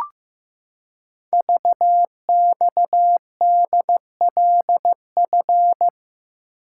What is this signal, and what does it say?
Morse code, heard as a keyed tone, about 15 wpm, 700 Hz. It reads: VXDLF
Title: Code Groups - 5 Letters - 15wpm